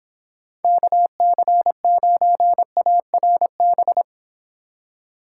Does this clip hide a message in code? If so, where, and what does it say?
Yes: Morse code from 0.6 to 4.0 s, heard as a keyed tone, about 26 wpm, 700 Hz. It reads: KC9ARB